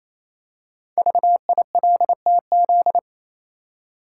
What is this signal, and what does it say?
Morse code, heard as a keyed tone, about 29 wpm, 700 Hz. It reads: VILTZ
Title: Code Groups - 5 Letters - 28wpm